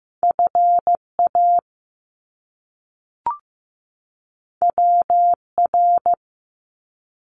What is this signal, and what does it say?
Morse code, heard as a keyed tone, about 15 wpm, 700 Hz. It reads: FA WR
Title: Top Two Letter Combinations in Words 15wpm